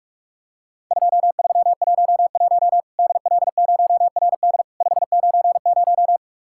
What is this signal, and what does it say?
Morse code, heard as a keyed tone, about 46 wpm, 700 Hz. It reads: 2311 DL0RD 590
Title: Worked All Europe DX Contest - QTC Exchange - 45wpm